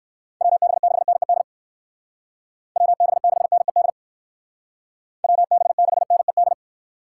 Morse code, heard as a keyed tone, about 53 wpm, 700 Hz. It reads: WB6NED WB6NED WB6NED
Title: Speed Racing - USA 2x3 Call Signs 45wpm